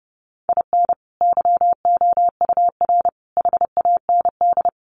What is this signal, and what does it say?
Morse code, heard as a keyed tone, about 30 wpm, 700 Hz. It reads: IN YOUR HAND
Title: Sentences from Top 1400 Words 30wpm